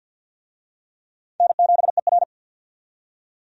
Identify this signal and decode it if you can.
Morse code, heard as a keyed tone, about 49 wpm, 700 Hz. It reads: N7ER
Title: USA Call Signs - 4 Characters - 50wpm